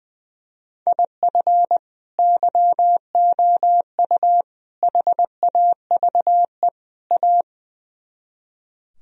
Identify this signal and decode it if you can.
Morse code, heard as a keyed tone, about 21 wpm, 700 Hz. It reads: IF YOU HAVE A
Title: Speed-Racing - Sets of 4 Words 20wpm